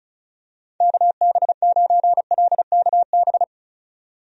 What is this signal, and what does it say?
Morse code, heard as a keyed tone, about 34 wpm, 700 Hz. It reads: KB9LKB
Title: USA Call Signs - 6 Characters - 35wpm